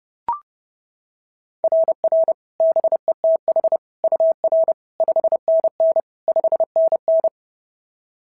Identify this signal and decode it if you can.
Morse code, heard as a keyed tone, about 30 wpm, 690 Hz. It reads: RR BETH UR 5NN 5NN
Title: SOTA QSO Practice 30wpm